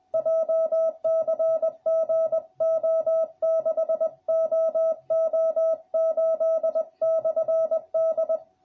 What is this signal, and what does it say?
Morse code, heard as a keyed tone, about 20 wpm, 650 Hz. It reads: JCG O6OO8/D